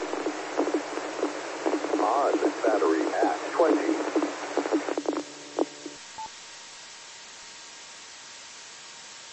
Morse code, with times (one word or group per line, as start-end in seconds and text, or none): none